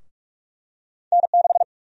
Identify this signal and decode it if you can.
Morse code, heard as a keyed tone, about 46 wpm, 700 Hz. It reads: N6
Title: USA Call Sign Prefixes - Repeated 2x - 45wpm